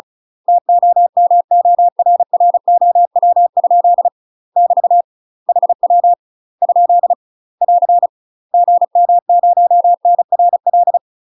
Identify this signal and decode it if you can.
Morse code, heard as a keyed tone, about 36 wpm, 700 Hz. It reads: TOMORROW? <BT> HW ? <AR> GM0NRL
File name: UK MASTER QSOs TRAINING / 35WPM / UK MASTER QSOs 0013